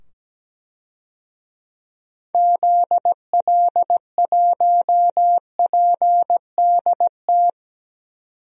Morse code, heard as a keyed tone, about 17 wpm, 700 Hz. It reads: ZL1PDT
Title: DXCC-100 Call Signs - 6 Characters - 17wpm